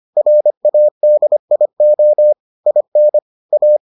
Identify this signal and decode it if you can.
Morse code, heard as a keyed tone, about 25 wpm, 600 Hz. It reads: RADIO IN A